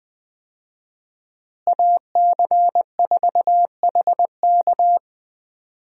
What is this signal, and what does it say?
Morse code, heard as a keyed tone, about 20 wpm, 700 Hz. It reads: AC4HK
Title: USA Call Signs - 5 Characters - Repeated 2x - 20wpm